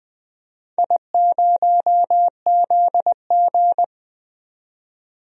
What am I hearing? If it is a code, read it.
Morse code, heard as a keyed tone, about 20 wpm, 700 Hz. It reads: I0ZG
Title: DXCC-100 Call Signs - 4 Characters - 20wpm